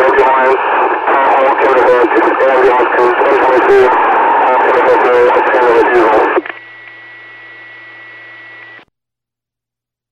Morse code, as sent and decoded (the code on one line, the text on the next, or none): none